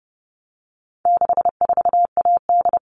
Morse code, heard as a keyed tone, about 30 wpm, 700 Hz. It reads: 64AD
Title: All Cut Numbers - 4-Digit 30wpm